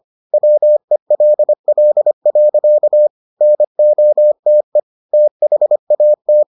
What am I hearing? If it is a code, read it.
Morse code, heard as a keyed tone, about 25 wpm, 600 Hz. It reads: WELL. NOTE THAT